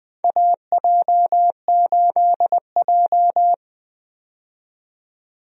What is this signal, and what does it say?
Morse code, heard as a keyed tone, about 20 wpm, 700 Hz. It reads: AJ8J
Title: USA Call Signs - 4 Characters - 20wpm